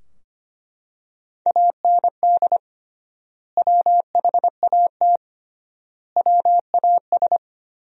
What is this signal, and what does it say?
Morse code, heard as a keyed tone, about 25 wpm, 700 Hz. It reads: AND WHAT WAS
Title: 3x Word-Spacing - Sentences from Top 100 Words - 25wpm